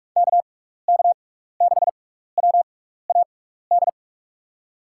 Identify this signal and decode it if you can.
Morse code, heard as a keyed tone, about 45 wpm, spaced as for 15 wpm, 700 Hz. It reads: KK6WAD
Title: USA Call Signs - 6 Characters - 6x Character Spacing - 45wpm